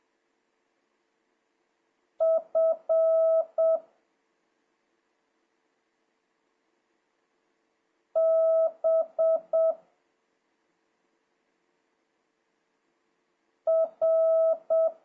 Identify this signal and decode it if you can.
Morse code, heard as a keyed tone, about 7 wpm, 1300 Hz. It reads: F B R
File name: jh8cbh / 2026cw